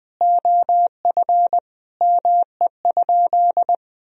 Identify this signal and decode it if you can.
Morse code, heard as a keyed tone, about 20 wpm, 700 Hz. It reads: OF ME?
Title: Sentences from Top 1000 Words - Review Part 4 - 20wpm